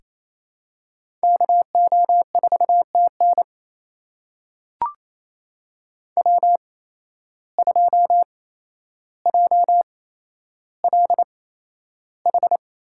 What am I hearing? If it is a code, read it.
Morse code, heard as a keyed tone, about 28 wpm, spaced as for 12 wpm, 700 Hz. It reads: KO4TN W2JLH
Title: USA Call Signs - 8x Character Spacing - 28wpm